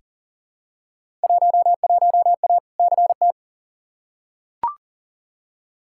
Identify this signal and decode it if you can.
Morse code, heard as a keyed tone, about 40 wpm, 700 Hz. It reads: 11A CT
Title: Field Day Exchanges Encore - 40wpm